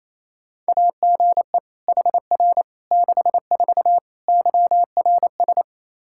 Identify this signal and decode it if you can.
Morse code, heard as a keyed tone, about 28 wpm, 700 Hz. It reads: AGE HR 64 YRS
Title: K1USN SST QSO Practice 28wpm